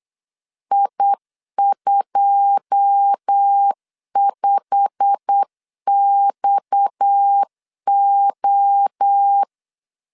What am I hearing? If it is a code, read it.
Morse code, heard as a keyed tone, about 8 wpm, 800 Hz. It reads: I25XO
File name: CW / DET / 1 9 1500C S C 8wpm 5653